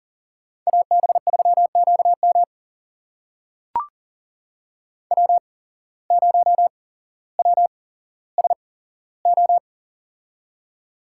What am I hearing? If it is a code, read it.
Morse code, heard as a keyed tone, about 40 wpm, spaced as for 17 wpm, 700 Hz. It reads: AB3QM W0WSO
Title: USA Call Signs - 5 Characters - 8x Character Spacing - 40wpm